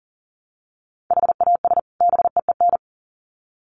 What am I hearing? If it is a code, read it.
Morse code, heard as a keyed tone, about 40 wpm, 700 Hz. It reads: HAS BEEN
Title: Sets of 2 Words 40wpm